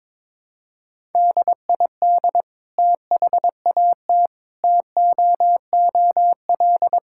Morse code, heard as a keyed tone, about 22 wpm, 700 Hz. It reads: DID THAT TOOL
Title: Sentences from Top 1000 Words - Review Part 6 - 22wpm